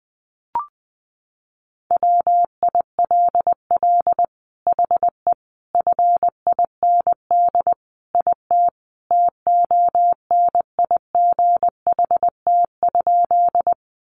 Morse code, heard as a keyed tone, about 19 wpm, 700 Hz. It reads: WILL HE FIND IT TONIGHT?
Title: Sentences from Top 600 Words 20wpm